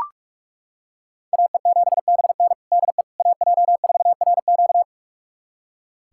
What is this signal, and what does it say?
Morse code, heard as a keyed tone, about 41 wpm, 700 Hz. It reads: AE7BN DE AJ4RQ
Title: USA Callsign DE Callsign - 5 Characters - 45wpm